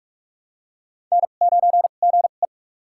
Morse code, heard as a keyed tone, about 46 wpm, spaced as for 32 wpm, 690 Hz. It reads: N9GE
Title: USA Call Signs - 4 Characters - 2x Character Spacing - 45wpm